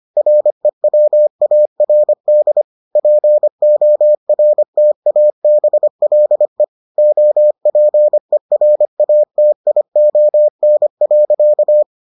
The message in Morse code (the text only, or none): REWARD PORTABLE OPERATION.